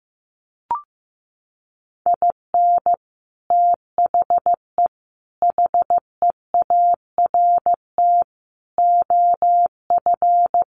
Morse code, heard as a keyed tone, about 15 wpm, 700 Hz. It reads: IN THE HEART OF
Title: Sets of 5 Words 15wpm